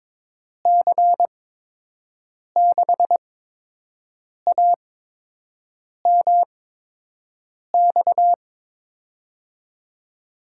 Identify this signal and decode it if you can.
Morse code, heard as a keyed tone, about 22 wpm, spaced as for 6 wpm, 700 Hz. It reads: C6AMX
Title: USA Call Signs - 8x Character Spacing - 22wpm